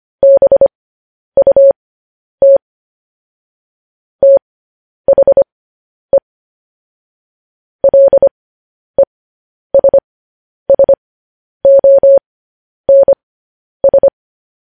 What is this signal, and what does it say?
Morse code, heard as a keyed tone, about 25 wpm, spaced as for 10 wpm, 550 Hz. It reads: BUT THE LESSONS